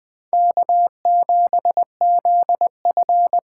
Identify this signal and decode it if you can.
Morse code, heard as a keyed tone, about 24 wpm, 700 Hz. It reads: K7ZF
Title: Speed Racing - USA 1x2 Call Signs 20wpm